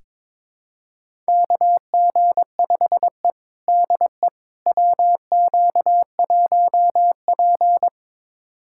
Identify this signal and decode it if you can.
Morse code, heard as a keyed tone, about 21 wpm, 700 Hz. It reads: KG5E DE WQ1P
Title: USA Callsign DE Callsign - 4 Characters - 22wpm